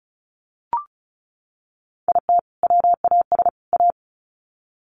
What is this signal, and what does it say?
Morse code, heard as a keyed tone, about 34 wpm, 700 Hz. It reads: IT WAS A
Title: Sets of 3 Words 35wpm